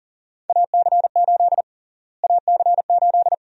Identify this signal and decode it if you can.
Morse code, heard as a keyed tone, about 40 wpm, 700 Hz. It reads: AC8 AC8
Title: USA Call Sign Prefixes - Repeated 2x - 40wpm